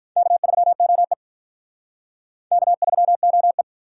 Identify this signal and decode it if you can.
Morse code, heard as a keyed tone, about 44 wpm, 700 Hz. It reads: K3OE K3OE